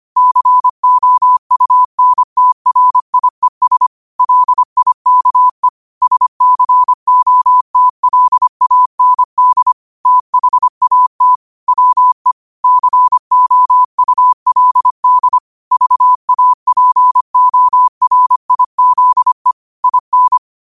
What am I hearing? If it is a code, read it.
Morse code, heard as a keyed tone, about 25 wpm, 1000 Hz. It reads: COUNTRIES LIKE SCOTLAND THAT WE COULD VAPORIZE IN
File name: qotdfiles / 25wpm